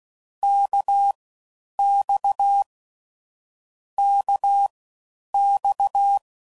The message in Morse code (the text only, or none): KX KX